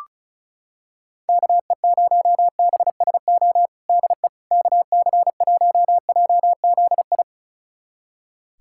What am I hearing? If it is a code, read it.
Morse code, heard as a keyed tone, about 36 wpm, 700 Hz. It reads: KE0BSO DE KC1JZI
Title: USA Callsign DE Callsign - 6 Characters - 35wpm